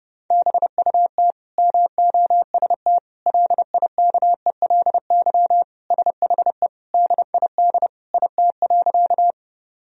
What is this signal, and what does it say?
Morse code, heard as a keyed tone, about 30 wpm, 700 Hz. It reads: BUT MOST LIKELY SHE DID IT.
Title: Sentences from Top 1000 Words - Review Part 4 - 30wpm